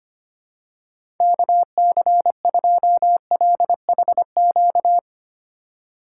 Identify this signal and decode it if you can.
Morse code, heard as a keyed tone, about 25 wpm, 700 Hz. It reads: KC2LHQ
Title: USA Call Signs - 6 Characters - 3x Character Spacing - 25wpm